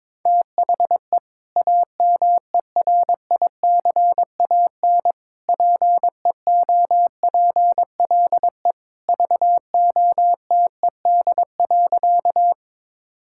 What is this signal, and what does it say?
Morse code, heard as a keyed tone, about 21 wpm, 700 Hz. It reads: THE AMERICAN PEOPLE VOTED.
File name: practice / 22wpm / Sentences with 3-Word Phrases - 22wpm